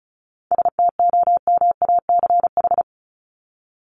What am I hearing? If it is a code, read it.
Morse code, heard as a keyed tone, about 34 wpm, 700 Hz. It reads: STOMACH
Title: Top 1501-1750 Words 35wpm